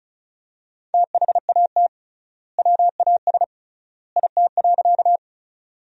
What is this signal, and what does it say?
Morse code, heard as a keyed tone, about 34 wpm, 700 Hz. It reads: THAT WAS IT.
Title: 3x Word-Spacing - Sentences from Top 100 Words - 35wpm